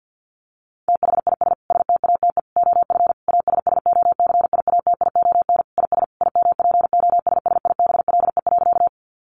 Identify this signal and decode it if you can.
Morse code, heard as a keyed tone, about 50 wpm, 700 Hz. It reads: THIS STATE OF ASSOCIATION IS IMPOSSIBLE.
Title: Sentences from Top 1100 Words 50wpm